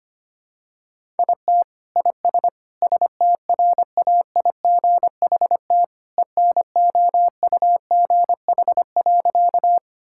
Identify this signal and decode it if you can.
Morse code, heard as a keyed tone, about 25 wpm, 700 Hz. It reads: IT IS STRAIGHT ENOUGH.
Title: Sentences from Top 1000 Words - Review Part 6 - 25wpm